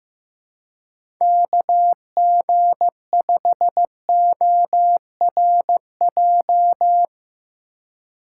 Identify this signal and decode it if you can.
Morse code, heard as a keyed tone, about 15 wpm, 700 Hz. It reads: KG5ORJ
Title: USA Call Signs - 6 Characters - 3x Character Spacing - 15wpm